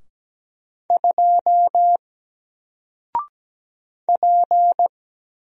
Morse code, heard as a keyed tone, about 17 wpm, 700 Hz. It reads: 2 P
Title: Single Character (Letter:Number:Punctuation:Prosign) Alt 17wpm